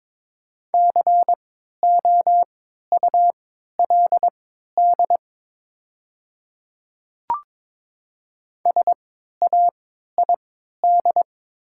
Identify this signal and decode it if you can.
Morse code, heard as a keyed tone, about 22 wpm, spaced as for 12 wpm, 700 Hz. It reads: COULD SAID
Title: 3x Character Spacing - Top 100 Words - 22wpm